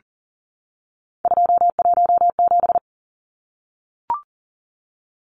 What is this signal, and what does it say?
Morse code, heard as a keyed tone, about 40 wpm, 700 Hz. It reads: 217
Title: Numbers - 3-Digit 40wpm